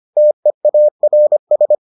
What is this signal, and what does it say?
Morse code, heard as a keyed tone, about 25 wpm, 600 Hz. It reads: TEARS